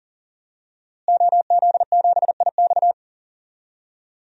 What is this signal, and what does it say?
Morse code, heard as a keyed tone, about 37 wpm, 700 Hz. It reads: OZ7IX
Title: DXCC-100 Call Signs - 5 Characters - Repeated 3x - 40wpm